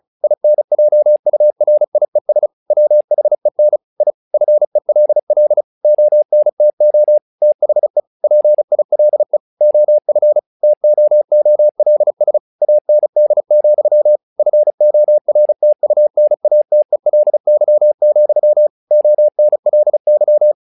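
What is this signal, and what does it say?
Morse code, heard as a keyed tone, about 36 wpm, 600 Hz. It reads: INJURIES WHEN I FELL ONTO THE PILE OF TOOLS AND, FORTUNATELY, ONLY
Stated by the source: Morse practice text "Tower Trouble" sent at 35 WPM, from the SKCC - Straight Key Century Club learning center audio files.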